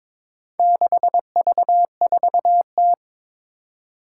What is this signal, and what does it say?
Morse code, heard as a keyed tone, about 22 wpm, 700 Hz. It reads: 6V4T
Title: Four Challenging Letter-Number Combinations 22wpm